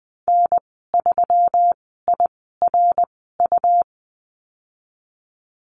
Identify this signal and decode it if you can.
Morse code, heard as a keyed tone, about 20 wpm, spaced as for 14 wpm, 700 Hz. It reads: N3IRU
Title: 179 Lesson 29g - Slash - Call Signs 20wpm